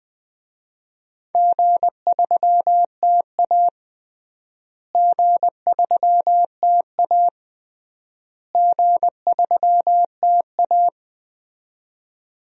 Morse code, heard as a keyed tone, about 21 wpm, 700 Hz. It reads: G3TA G3TA G3TA
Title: DXCC-100 Call Signs - 4 Characters - Repeated 3x - 20wpm